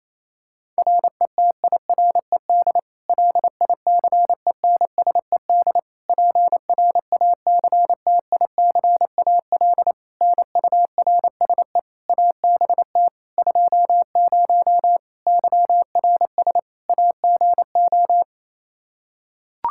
Morse code, heard as a keyed tone, about 28 wpm, 700 Hz. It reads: RETIRED LICENSED PRACTICAL NURSE ABT 20 YRS AGO
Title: Practice QSO - Exchange 3 - More Info - 28wpm